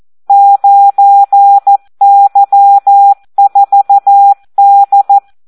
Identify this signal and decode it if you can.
Morse code, heard as a keyed tone, about 14 wpm, 800 Hz. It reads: 9Y4D
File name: Audio practica CW / SD 1 14